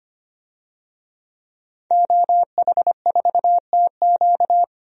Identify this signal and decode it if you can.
Morse code, heard as a keyed tone, about 25 wpm, 700 Hz. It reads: OH4TQ
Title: DXCC-100 Call Signs - 5 Characters - 25wpm